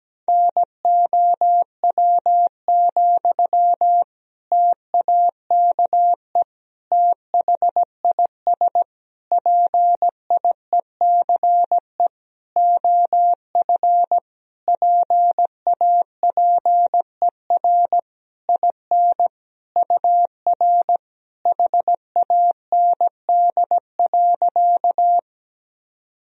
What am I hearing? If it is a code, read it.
Morse code, heard as a keyed tone, about 17 wpm, 700 Hz. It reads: NOW, TAKE THIS PIECE OF PAPER IN UR HAND.
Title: Sentences from Top 400 Words and Abbreviations 17wpm